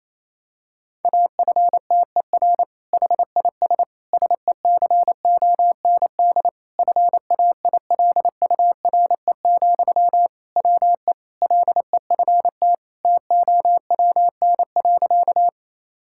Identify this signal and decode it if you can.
Morse code, heard as a keyed tone, about 28 wpm, 700 Hz. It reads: AFTER HIS SECOND FAILURE, WE LEFT TOWN.
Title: Sentences from Top 1000 Words - Review Part 6 - 28wpm